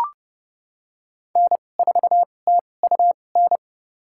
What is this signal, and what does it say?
Morse code, heard as a keyed tone, about 30 wpm, spaced as for 22 wpm, 700 Hz. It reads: N4TUN
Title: USA Call Signs - 5 Characters - 2x Character Spacing - 30wpm